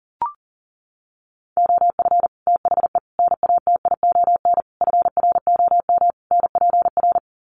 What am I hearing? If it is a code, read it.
Morse code, heard as a keyed tone, about 37 wpm, 700 Hz. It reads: OF THE NATION FROM NPR
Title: Sets of 5 Words 40wpm